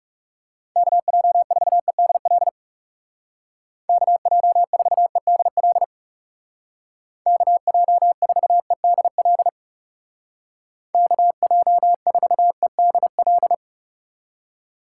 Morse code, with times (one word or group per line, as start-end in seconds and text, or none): 0.8-2.5 KJ4EDL
3.9-5.8 KJ4EDL
7.3-9.5 KJ4EDL
10.9-13.5 KJ4EDL